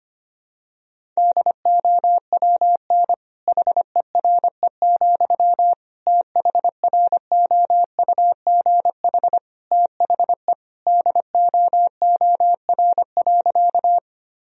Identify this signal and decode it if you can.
Morse code, heard as a keyed tone, about 25 wpm, 700 Hz. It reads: DOWN HERE, THROUGH THE DOOR.
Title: Sentences from Top 1000 Words - Review Part 1 - 25wpm